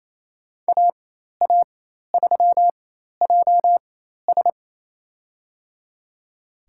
Morse code, heard as a keyed tone, about 28 wpm, spaced as for 13 wpm, 700 Hz. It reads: AA3JS
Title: USA Call Signs - 5 Characters - 4x Character Spacing - 28wpm